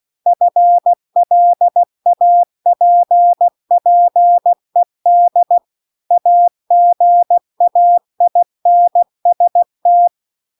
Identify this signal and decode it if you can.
Morse code, heard as a keyed tone, about 16 wpm, 700 Hz. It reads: FLAPPED AGAINST
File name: CW-The-Big-Twoheaded-River-2-16WPM